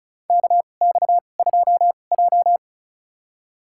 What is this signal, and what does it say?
Morse code, heard as a keyed tone, about 36 wpm, spaced as for 25 wpm, 700 Hz. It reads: KX2J
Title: USA Call Signs - 4 Characters - 2x Character Spacing - 35wpm